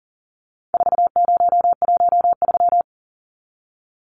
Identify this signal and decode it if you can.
Morse code, heard as a keyed tone, about 40 wpm, 700 Hz. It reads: 4013
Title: Numbers - 4-Digit 40wpm